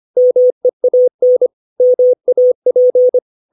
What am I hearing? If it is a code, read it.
Morse code, heard as a keyed tone, about 25 wpm, 500 Hz. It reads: MEAN MAP
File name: 632 words / 632 Words 25WPM